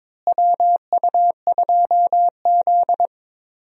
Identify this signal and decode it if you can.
Morse code, heard as a keyed tone, about 21 wpm, 700 Hz. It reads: WU2Z
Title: USA Call Signs - 4 Characters - Repeated 3x - 22wpm